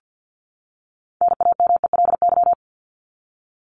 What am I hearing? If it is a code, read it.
Morse code, heard as a keyed tone, about 50 wpm, 700 Hz. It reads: NAMELY
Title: LY Suffix - Top 100 Words with Base Words 50wpm